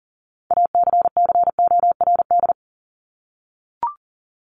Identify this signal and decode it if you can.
Morse code, heard as a keyed tone, about 40 wpm, 700 Hz. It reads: ACCORD
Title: ING Suffix - Top 100 Words with Base Words 40wpm